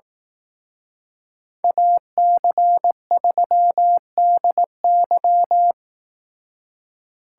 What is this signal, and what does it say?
Morse code, heard as a keyed tone, about 18 wpm, 700 Hz. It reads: AC3DY
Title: Speed Racing - USA 2x2 Call Signs 15wpm